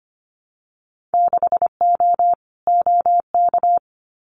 Morse code, heard as a keyed tone, about 25 wpm, 690 Hz. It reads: 6O OK
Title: Winter Field Day Exchanges Encore 25wpm